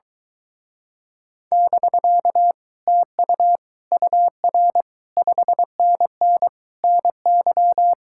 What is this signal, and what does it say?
Morse code, heard as a keyed tone, about 24 wpm, 700 Hz. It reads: <BK> TU UR 5NN NY